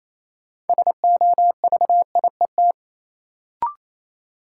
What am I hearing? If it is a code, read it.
Morse code, heard as a keyed tone, about 28 wpm, 700 Hz. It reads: SOVIET
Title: Sets of 5 Words Encore - New Word Drill - 28wpm